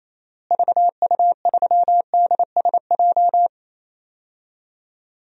Morse code, heard as a keyed tone, about 28 wpm, 700 Hz. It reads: VU3DSJ
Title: DXCC-100 Call Signs - 6 Characters - Repeated 3x - 28wpm